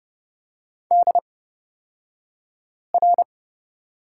Morse code, heard as a keyed tone, about 30 wpm, spaced as for 5 wpm, 700 Hz. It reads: DR
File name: Lesson 38d - Letter q - Two Characters 30wpm